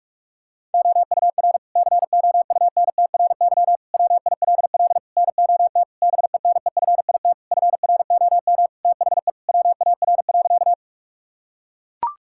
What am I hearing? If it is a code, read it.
Morse code, heard as a keyed tone, about 41 wpm, 700 Hz. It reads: OUR COUNTRY WILL NOT BENEFIT FROM THE WAR.
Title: Sentences from Top 1000 Words - Review Part 6 - 45wpm